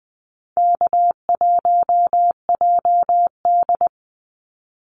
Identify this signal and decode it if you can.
Morse code, heard as a keyed tone, about 20 wpm, 700 Hz. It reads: K1JD
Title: NA SOTA Activator Callsigns 20wpm